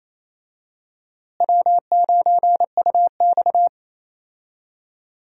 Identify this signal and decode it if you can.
Morse code, heard as a keyed tone, about 28 wpm, 700 Hz. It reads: W9UX
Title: USA Call Signs - 4 Characters - Repeated 3x - 28wpm